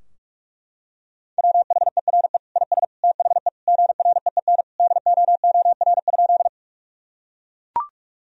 Morse code, heard as a keyed tone, about 41 wpm, 700 Hz. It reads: WHERE IS THE GREEN DOOR?
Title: Sentences from Top 1000 Words - Review Part 2 - 45wpm